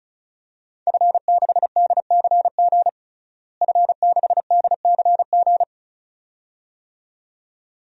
Morse code, heard as a keyed tone, about 36 wpm, 700 Hz. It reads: F6DCG F6DCG
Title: DXCC-100 Call Signs - 5 Characters - Repeated 2x - 35wpm